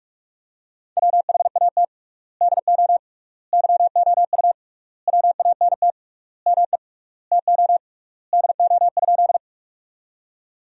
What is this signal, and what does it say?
Morse code, heard as a keyed tone, about 41 wpm, 700 Hz. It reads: WHAT DO YOU WANT ME TO DO?
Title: 3x Word-Spacing - Sentences from Top 200 Words - 45wpm